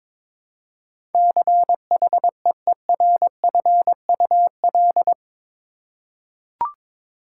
Morse code, heard as a keyed tone, about 22 wpm, 700 Hz. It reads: CHEERFUL
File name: practice / 22wpm / Eight Letter Words - 22wpm